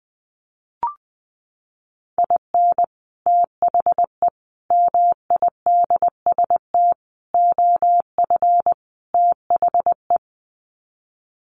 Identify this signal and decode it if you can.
Morse code, heard as a keyed tone, about 20 wpm, 700 Hz. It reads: IN THE MIDST OF THE
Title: Sets of 5 Words 20wpm